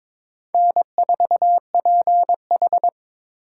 Morse code, heard as a keyed tone, about 22 wpm, 700 Hz. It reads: N4PH
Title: USA Call Signs - 4 Characters - 2x Character Spacing - 22wpm